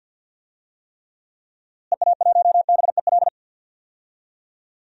Morse code, heard as a keyed tone, about 52 wpm, 700 Hz. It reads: EA1BEL
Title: DXCC-100 Call Signs - 6 Characters - Repeated 2x - 50wpm